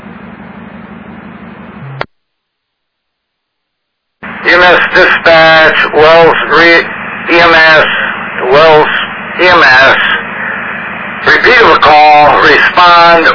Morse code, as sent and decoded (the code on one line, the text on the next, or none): none